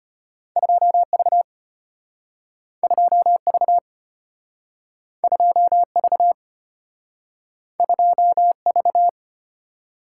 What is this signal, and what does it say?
Morse code, heard as a keyed tone, about 33 wpm, 700 Hz. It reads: 2V 2V 2V 2V